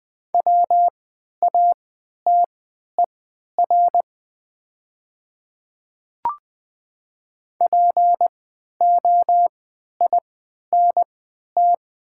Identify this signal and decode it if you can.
Morse code, heard as a keyed tone, about 20 wpm, spaced as for 11 wpm, 700 Hz. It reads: WATER POINT
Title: 3x Character Spacing - Top 200 Words - 20wpm